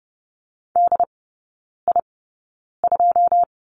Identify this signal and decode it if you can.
Morse code, heard as a keyed tone, about 30 wpm, spaced as for 9 wpm, 700 Hz. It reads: DI2
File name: Lesson 16e - Letter c - Three Characters 30wpm